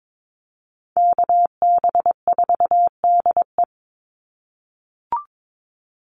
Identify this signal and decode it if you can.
Morse code, heard as a keyed tone, about 22 wpm, 700 Hz. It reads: KB4DE
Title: Call Signs - 22wpm